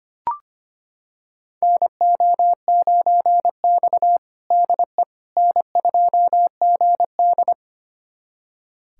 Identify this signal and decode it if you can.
Morse code, heard as a keyed tone, about 25 wpm, 700 Hz. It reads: NO9X DE N2GD
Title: USA Callsign DE Callsign - 4 Characters - 25wpm